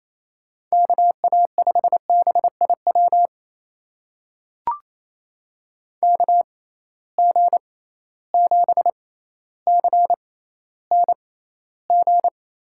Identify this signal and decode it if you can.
Morse code, heard as a keyed tone, about 28 wpm, spaced as for 14 wpm, 700 Hz. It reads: KA5BIW KG7CNG
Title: USA Call Signs - 6 Characters - 6x Character Spacing - 28wpm